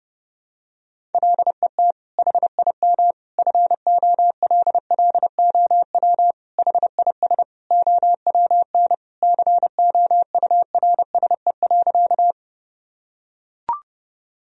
Morse code, heard as a keyed tone, about 30 wpm, 700 Hz. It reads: LET HIM FOLLOW HIS OWN COURSE.
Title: Sentences from Top 1000 Words - Review Part 1 - 30wpm